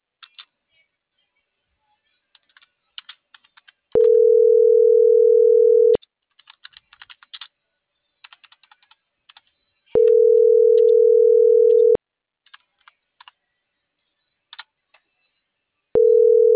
Ringback tone (North America), 440 + 480 Hz.